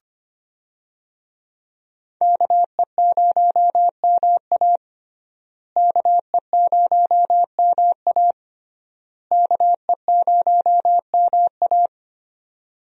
Morse code, heard as a keyed tone, about 25 wpm, 700 Hz. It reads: KE0MA KE0MA KE0MA
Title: USA Call Signs - 5 Characters - Repeated 3x - 25wpm